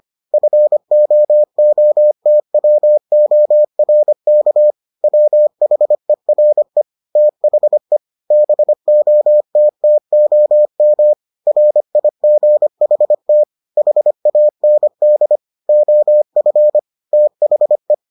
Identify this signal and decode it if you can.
Morse code, heard as a keyed tone, about 25 wpm, 600 Hz. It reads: FOOTWORK WHERE THE BOTTOM RIGHT HAND OF THE